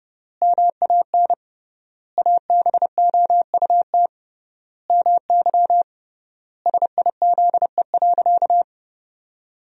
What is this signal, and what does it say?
Morse code, heard as a keyed tone, about 30 wpm, 700 Hz. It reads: MAN ABOUT MY SIZE.
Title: 3x Word-Spacing - Sentences from Top 400 Words - 30wpm